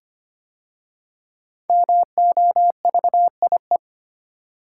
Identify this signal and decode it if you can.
Morse code, heard as a keyed tone, about 25 wpm, 700 Hz. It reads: MOVIE
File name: 5 Letter Words - First 3 Letters Given - 25wpm